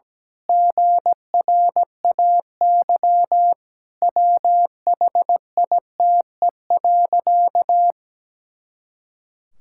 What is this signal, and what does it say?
Morse code, heard as a keyed tone, about 17 wpm, 700 Hz. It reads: GRAY WHITE.
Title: Pangram Challenge v2 - 17wpm